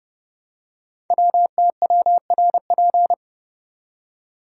Morse code, heard as a keyed tone, about 30 wpm, 700 Hz. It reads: WTWRP
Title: Code Groups - 5 Letters - 30wpm